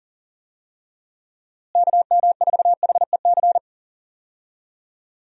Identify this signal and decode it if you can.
Morse code, heard as a keyed tone, about 39 wpm, 700 Hz. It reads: KM4HEC
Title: USA Call Signs - 6 Characters - Repeated 2x - 40wpm